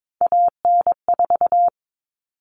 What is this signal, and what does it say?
Morse code, heard as a keyed tone, about 22 wpm, 700 Hz. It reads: AN4
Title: All Cut Numbers - 3-Digit 22wpm